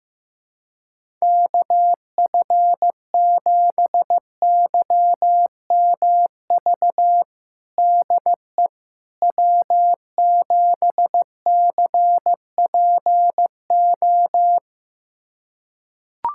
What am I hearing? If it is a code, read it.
Morse code, heard as a keyed tone, about 15 wpm, 700 Hz. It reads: KF7YMV DE W7CPO
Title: USA Callsign DE Callsign 15wpm